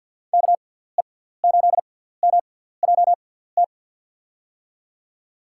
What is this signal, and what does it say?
Morse code, heard as a keyed tone, about 52 wpm, spaced as for 17 wpm, 700 Hz. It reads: KE8MJT
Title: USA Call Signs - 6 Characters - 6x Character Spacing - 50wpm